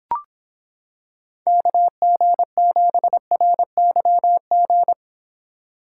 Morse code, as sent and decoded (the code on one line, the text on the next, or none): -.- --. --... .-. -.-- --.
KG7RYG